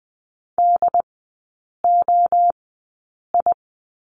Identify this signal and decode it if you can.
Morse code, heard as a keyed tone, about 20 wpm, spaced as for 8 wpm, 690 Hz. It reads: DOI
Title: 159 Lesson 26e - Letter g - Three Characters 20wpm